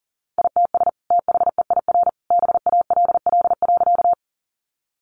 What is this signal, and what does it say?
Morse code, heard as a keyed tone, about 37 wpm, 700 Hz. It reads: ITS THEIR BALL.
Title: Sentences from Top 900 Words 40wpm